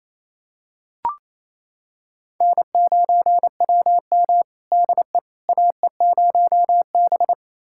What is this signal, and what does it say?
Morse code, heard as a keyed tone, about 28 wpm, 700 Hz. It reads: N9WM DE AE0B
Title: USA Callsign DE Callsign - 4 Characters - 28wpm